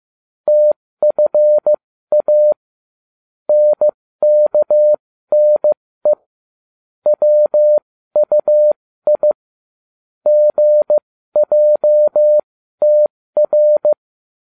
Morse code, heard as a keyed tone, about 15 wpm, spaced as for 13 wpm, 600 Hz. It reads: TFA NKNE WUI GJTR